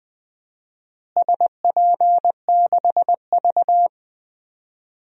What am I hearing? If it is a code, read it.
Morse code, heard as a keyed tone, about 20 wpm, 700 Hz. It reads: SP6V
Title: DXCC-100 Call Signs - 4 Characters - Repeated 3x - 20wpm